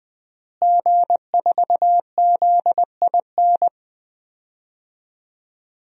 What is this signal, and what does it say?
Morse code, heard as a keyed tone, about 21 wpm, 700 Hz. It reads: G4ZIN
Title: DXCC-100 Call Signs - 5 Characters - Repeated 2x - 20wpm